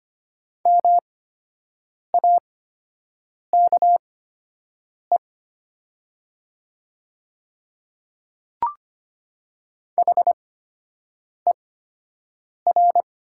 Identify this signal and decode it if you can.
Morse code, heard as a keyed tone, about 25 wpm, spaced as for 7 wpm, 700 Hz. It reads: MAKE HER
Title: 8x Character Spacing - Top 200 Words - 25wpm